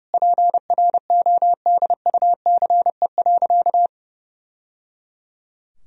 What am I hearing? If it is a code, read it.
Morse code, heard as a keyed tone, about 30 wpm, 700 Hz. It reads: PRODUCE.
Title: Sentences from Top 1000 Words - Review Part 3 - 30wpm